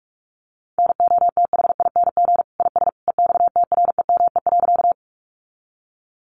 Morse code, heard as a keyed tone, about 44 wpm, 700 Hz. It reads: NOTHING IS EXTREME.
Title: Sentences from Top 1300 Words 45wpm